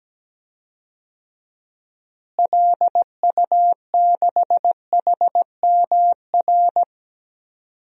Morse code, heard as a keyed tone, about 17 wpm, 700 Hz. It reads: LU6HMR
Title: DXCC-100 Call Signs - 6 Characters - Repeated 3x - 17wpm